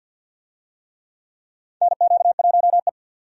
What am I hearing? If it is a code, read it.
Morse code, heard as a keyed tone, about 52 wpm, 700 Hz. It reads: NQ1E
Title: USA Call Signs - 4 Characters - 50wpm